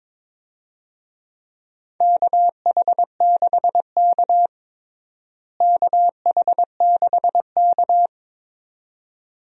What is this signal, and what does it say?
Morse code, heard as a keyed tone, about 22 wpm, 700 Hz. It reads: KH6K KH6K
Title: DXCC-100 Call Signs - 4 Characters - Repeated 2x - 22wpm